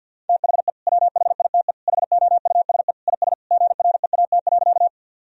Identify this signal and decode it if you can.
Morse code, heard as a keyed tone, about 48 wpm, 700 Hz. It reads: THE WHITE HOUSE IS GREAT.
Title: Sentences from Top 1000 Words - Review Part 1 - 50wpm